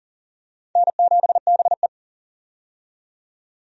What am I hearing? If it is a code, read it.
Morse code, heard as a keyed tone, about 40 wpm, 700 Hz. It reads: N7BE
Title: USA Call Signs - 4 Characters - 40wpm